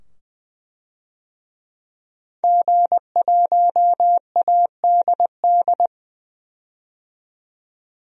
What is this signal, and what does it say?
Morse code, heard as a keyed tone, about 20 wpm, 700 Hz. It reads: G1ADD
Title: DXCC-100 Call Signs - 5 Characters - 20wpm